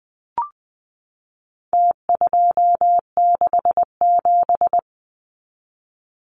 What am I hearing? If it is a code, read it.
Morse code, heard as a keyed tone, about 20 wpm, 700 Hz. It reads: T267
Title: Numbers - Contest Style - 4-Digit 20wpm